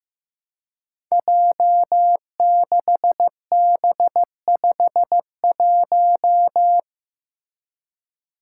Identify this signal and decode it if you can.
Morse code, heard as a keyed tone, about 15 wpm, 700 Hz. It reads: J6B51
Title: Five Challenging Letter-Number Combinations 15wpm